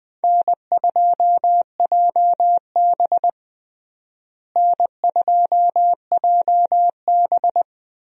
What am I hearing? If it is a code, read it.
Morse code, heard as a keyed tone, about 20 wpm, 700 Hz. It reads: N2JB N2JB
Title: USA Call Signs - 4 Characters - Repeated 2x - 20wpm